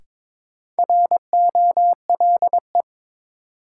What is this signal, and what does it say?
Morse code, heard as a keyed tone, about 22 wpm, 700 Hz. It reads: ROLE
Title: Sentences from Top 1500 Words - New Word Drill Alt - 22wpm